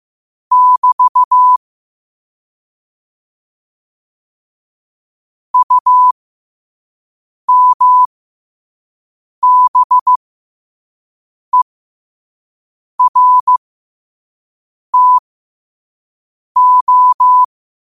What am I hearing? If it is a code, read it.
Morse code, heard as a keyed tone, about 15 wpm, spaced as for 5 wpm, 1000 Hz. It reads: <BT> UMBERTO
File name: qotdfiles / 5wpm